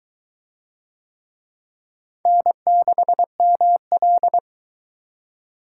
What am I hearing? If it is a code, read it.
Morse code, heard as a keyed tone, about 22 wpm, 700 Hz. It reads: N6ML